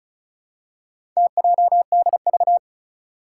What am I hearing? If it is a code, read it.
Morse code, heard as a keyed tone, about 36 wpm, 700 Hz. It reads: TJDV
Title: Four Challenging Letter-Number Combinations 35wpm